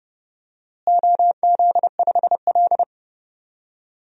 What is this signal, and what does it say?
Morse code, heard as a keyed tone, about 30 wpm, 700 Hz. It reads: OZ5L
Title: DXCC-100 Call Signs - 4 Characters - Repeated 3x - 30wpm